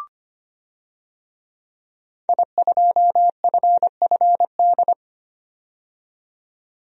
Morse code, heard as a keyed tone, about 25 wpm, 700 Hz. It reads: I2FFD
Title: DXCC-100 Call Signs - 5 Characters - 25wpm